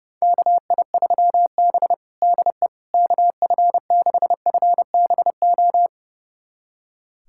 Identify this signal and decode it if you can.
Morse code, heard as a keyed tone, about 32 wpm, 700 Hz. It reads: KI3B DE KF6FBO
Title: USA Callsign DE Callsign 30wpm